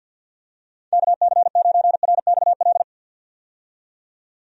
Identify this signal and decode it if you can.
Morse code, heard as a keyed tone, about 47 wpm, 700 Hz. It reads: KC9RXL
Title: USA Call Signs - 6 Characters - 50wpm